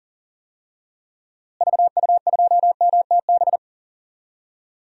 Morse code, heard as a keyed tone, about 40 wpm, 700 Hz. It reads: VU2MTB
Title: DXCC-100 Call Signs - 6 Characters - Repeated 3x - 40wpm